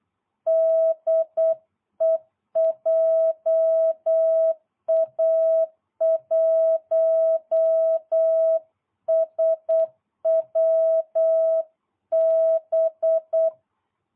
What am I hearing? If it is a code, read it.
Morse code, heard as a keyed tone, about 8 wpm, 660 Hz. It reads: DE JA1SWB